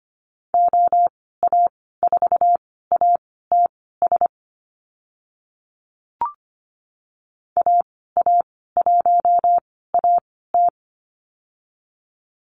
Morse code, heard as a keyed tone, about 25 wpm, spaced as for 16 wpm, 700 Hz. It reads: OA4ATS AA1AT
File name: Lesson 15g - Number 5 - Call Signs 25wpm